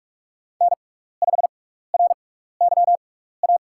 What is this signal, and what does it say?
Morse code, heard as a keyed tone, about 43 wpm, spaced as for 15 wpm, 700 Hz. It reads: N5RYA
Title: USA Call Signs - 5 Characters - 6x Character Spacing - 45wpm